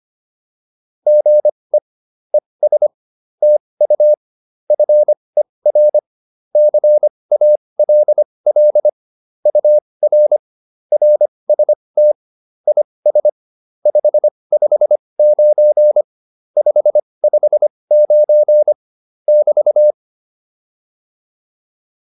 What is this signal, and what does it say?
Morse code, heard as a keyed tone, about 25 wpm, spaced as for 20 wpm, 600 Hz. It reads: GE ES TU FER CALL UR RST IS 559 559 <BT>